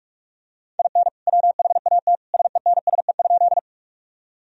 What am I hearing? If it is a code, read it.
Morse code, heard as a keyed tone, about 46 wpm, 700 Hz. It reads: IN WHAT SENSE?
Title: Sentences from Top 1000 Words - Review Part 4 - 45wpm